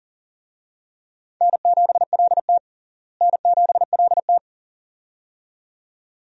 Morse code, heard as a keyed tone, about 37 wpm, 700 Hz. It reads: N7LT N7LT
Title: USA Call Signs - 4 Characters - Repeated 2x - 40wpm